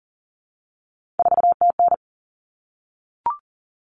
Morse code, heard as a keyed tone, about 40 wpm, 700 Hz. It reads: 4TN
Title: Common Cut Numbers - 3-Digit 40wpm